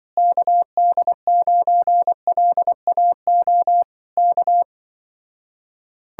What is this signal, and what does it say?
Morse code, heard as a keyed tone, about 25 wpm, 700 Hz. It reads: KD9LAO K
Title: Speed-Racing Common QSO Phrases - 20wpm